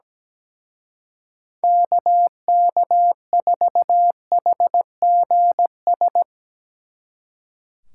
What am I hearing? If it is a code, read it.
Morse code, heard as a keyed tone, about 17 wpm, 700 Hz. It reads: KK4HGS
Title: Speed Racing - USA 2x3 Call Signs 17wpm